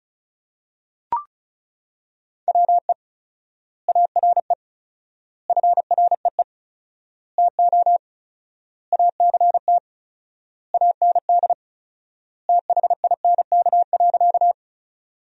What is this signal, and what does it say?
Morse code, heard as a keyed tone, about 34 wpm, 700 Hz. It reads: WE ARE FREE TO ACT AND THINK.
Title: 4x Word-Spacing - Sentences from Top 500 Words - 35wpm